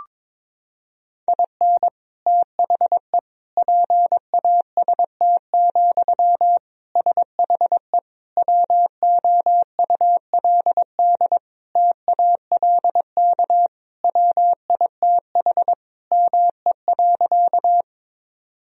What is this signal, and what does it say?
Morse code, heard as a keyed tone, about 22 wpm, 700 Hz. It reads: IN THE PAST, SHE WOULD TALK WITH ME.
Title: Sentences from Top 1000 Words - Review Part 3 - 22wpm